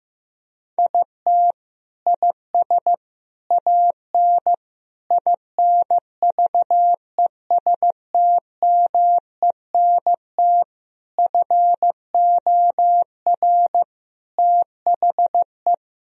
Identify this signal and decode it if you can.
Morse code, heard as a keyed tone, about 15 wpm, 700 Hz. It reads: IT IS AN INVESTMENT FOR THE
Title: Sentences from Top 1000 Words - Review Part 4 - 15wpm